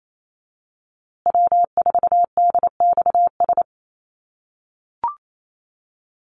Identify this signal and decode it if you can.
Morse code, heard as a keyed tone, about 28 wpm, 700 Hz. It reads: W4DXS
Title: USA Call Signs Repeated 2x 28wpm